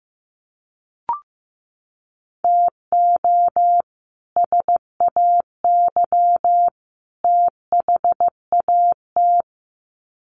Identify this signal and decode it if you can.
Morse code, heard as a keyed tone, about 15 wpm, 700 Hz. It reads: TO SAY THAT
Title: Sets of 3 Words 15wpm